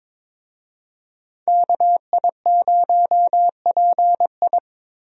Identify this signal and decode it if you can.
Morse code, heard as a keyed tone, about 23 wpm, 700 Hz. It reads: KI0PI
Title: USA Call Signs - 5 Characters - Repeated 2x - 22wpm